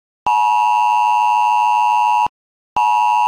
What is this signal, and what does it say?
Emergency Alert System (SAME) tones (the 853 and 960 Hz attention tone), with noise. It is heard from 0.2 to 2.3 s.
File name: Smart-Folder-X100S / ui / emergency alert